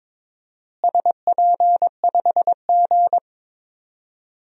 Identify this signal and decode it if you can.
Morse code, heard as a keyed tone, about 22 wpm, 700 Hz. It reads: SP5G
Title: DXCC-100 Call Signs - 4 Characters - Repeated 3x - 22wpm